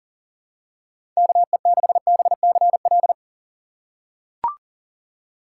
Morse code, heard as a keyed tone, about 40 wpm, 700 Hz. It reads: KE6BCL
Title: USA Call Signs - 3x Character Spacing - 40wpm